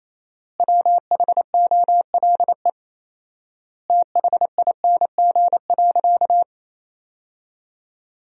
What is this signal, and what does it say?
Morse code, heard as a keyed tone, about 27 wpm, 700 Hz. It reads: WHOLE THING.
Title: 4x Word-Spacing - Sentences from Top 400 Words - 28wpm